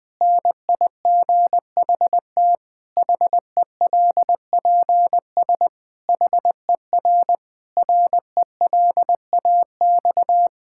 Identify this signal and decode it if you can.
Morse code, heard as a keyed tone, about 20 wpm, 700 Hz. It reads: NIGHT HELPS HER RELAX
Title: Difficult Sentences with Hints - 20wpm